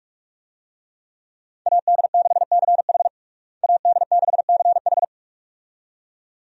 Morse code, heard as a keyed tone, about 41 wpm, 700 Hz. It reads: AD6CH AD6CH
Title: USA Call Signs - 5 Characters - Repeated 2x - 45wpm